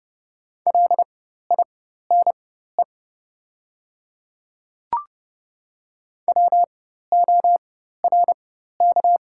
Morse code, heard as a keyed tone, about 30 wpm, spaced as for 14 wpm, 700 Hz. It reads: LINE WORK
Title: 4x Character Spacing - Top 200 Words - 30wpm